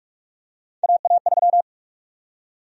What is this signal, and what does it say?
Morse code, heard as a keyed tone, about 46 wpm, 700 Hz. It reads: AA3
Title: USA Call Sign Prefixes - 45wpm